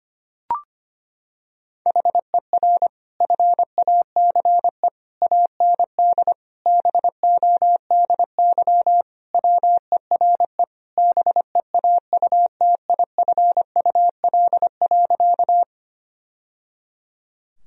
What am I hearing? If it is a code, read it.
Morse code, heard as a keyed tone, about 25 wpm, 700 Hz. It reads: HER FACE AND BODY WERE BEAUTIFUL.
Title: Sentences from Top 1000 Words - Review Part 2 - 25wpm